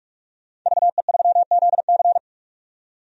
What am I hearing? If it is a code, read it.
Morse code, heard as a keyed tone, about 44 wpm, 690 Hz. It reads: VE3ZC
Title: DXCC-100 Call Signs - 5 Characters - 45wpm